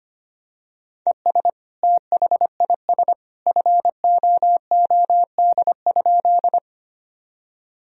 Morse code, heard as a keyed tone, about 25 wpm, 700 Hz. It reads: ES THIS FOOD?
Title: Sentences from Top 300 Words and Abbreviations 25wpm